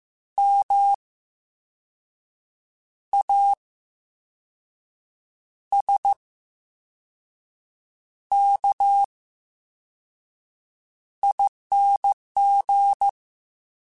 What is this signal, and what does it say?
Morse code, heard as a keyed tone, about 15 wpm, spaced as for 5 wpm, 800 Hz. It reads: MASKING
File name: Special INGs / Special INGs-mp3s / Slow ing @13